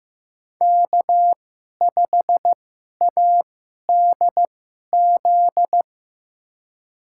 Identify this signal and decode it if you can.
Morse code, heard as a keyed tone, about 15 wpm, spaced as for 11 wpm, 700 Hz. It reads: K5ADZ
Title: USA Call Signs - 5 Characters - 2x Character Spacing - 15wpm